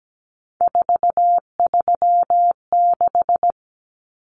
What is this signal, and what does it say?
Morse code, heard as a keyed tone, about 17 wpm, 700 Hz. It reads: 436
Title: Common Cut Numbers - 3-Digit 17wpm